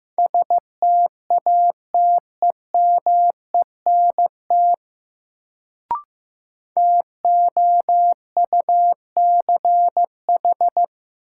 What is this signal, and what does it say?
Morse code, heard as a keyed tone, about 15 wpm, 700 Hz. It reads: STATEMENT TOUCH
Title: Sentences from Top 1200 Words - New Word Drill - 15wpm